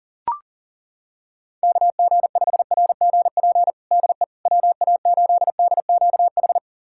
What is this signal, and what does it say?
Morse code, heard as a keyed tone, about 40 wpm, 700 Hz. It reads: KG5RGP DE WA8DQH
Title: USA Callsign DE Callsign 40wpm